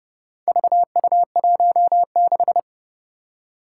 Morse code, heard as a keyed tone, about 30 wpm, 700 Hz. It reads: VU16
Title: Four Challenging Letter-Number Combinations 30wpm